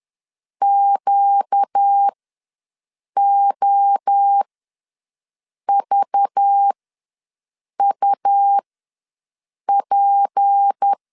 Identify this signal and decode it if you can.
Morse code, heard as a keyed tone, about 10 wpm, spaced as for 6 wpm, 800 Hz. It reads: QOVUP
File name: LEZIONI / 7 GF9Z / 7 GF ws 10wpm 3918